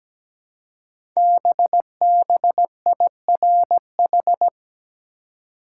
Morse code, heard as a keyed tone, about 17 wpm, 700 Hz. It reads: BBIRH
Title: Code Groups - 5 Letters - 17wpm